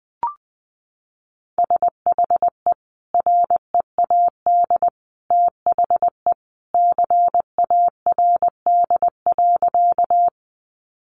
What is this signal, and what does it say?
Morse code, heard as a keyed tone, about 20 wpm, 700 Hz. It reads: SHE READ THE CARD.
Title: Sentences from Top 600 Words 20wpm